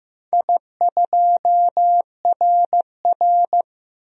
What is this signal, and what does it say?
Morse code, heard as a keyed tone, about 15 wpm, 700 Hz. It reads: I2RR
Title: DXCC-100 Call Signs - 4 Characters - 15wpm